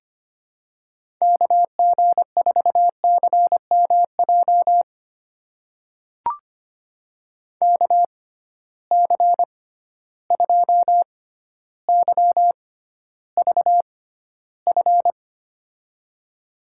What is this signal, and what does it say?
Morse code, heard as a keyed tone, about 25 wpm, spaced as for 13 wpm, 700 Hz. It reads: KG4CMJ KC2YVF
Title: USA Call Signs - 6 Characters - 6x Character Spacing - 25wpm